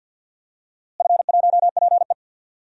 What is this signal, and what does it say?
Morse code, heard as a keyed tone, about 51 wpm, 700 Hz. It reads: F1PE
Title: DXCC-100 Call Signs - 4 Characters - Repeated 2x - 50wpm